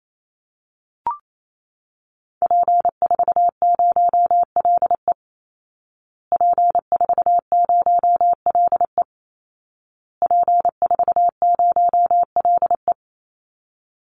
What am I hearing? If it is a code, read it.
Morse code, heard as a keyed tone, about 28 wpm, 700 Hz. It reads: P40LE P40LE P40LE
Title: Call Signs Repeated 2x - 28wpm